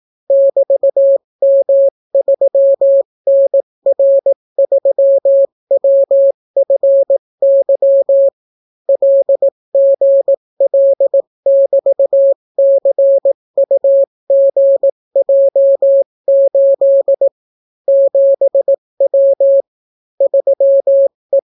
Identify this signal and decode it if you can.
Morse code, heard as a keyed tone, about 18 wpm, 550 Hz. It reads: <BT>M3NR3WFY LGL<BT>CUGJ8 7W 3E